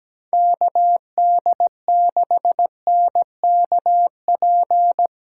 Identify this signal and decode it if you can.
Morse code, heard as a keyed tone, about 17 wpm, 700 Hz. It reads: KD6NKP
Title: USA Callsign DE Callsign 17wpm